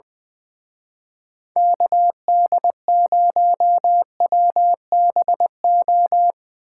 Morse code, heard as a keyed tone, about 22 wpm, 700 Hz. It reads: KD0WBO
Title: Speed Racing - USA Call Signs 20wpm